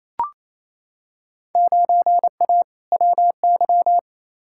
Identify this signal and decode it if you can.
Morse code, heard as a keyed tone, about 28 wpm, 700 Hz. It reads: 9A WY
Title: Field Day Exchanges Encore - 28wpm